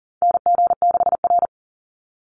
Morse code, heard as a keyed tone, about 40 wpm, 700 Hz. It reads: NG6R
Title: NA SOTA Chaser Callsigns 40wpm